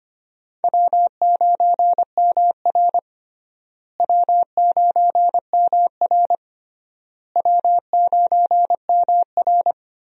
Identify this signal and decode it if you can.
Morse code, heard as a keyed tone, about 25 wpm, 700 Hz. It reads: W9MR W9MR W9MR
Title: USA Call Signs - 4 Characters - Repeated 3x - 25wpm